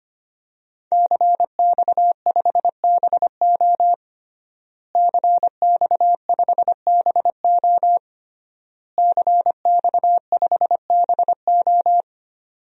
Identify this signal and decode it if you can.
Morse code, heard as a keyed tone, about 25 wpm, 700 Hz. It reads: CX5BO CX5BO CX5BO
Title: DXCC-100 Call Signs - 5 Characters - Repeated 3x - 25wpm